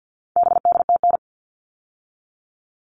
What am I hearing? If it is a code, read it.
Morse code, heard as a keyed tone, about 50 wpm, 700 Hz. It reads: BDTN
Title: All Cut Numbers - 4-Digit 50wpm